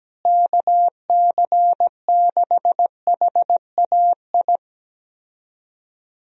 Morse code, heard as a keyed tone, about 17 wpm, 700 Hz. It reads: KC6HAI
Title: USA Call Signs - 6 Characters - Repeated 3x - 17wpm